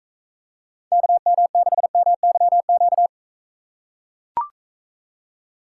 Morse code, heard as a keyed tone, about 41 wpm, 700 Hz. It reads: KM6MYQ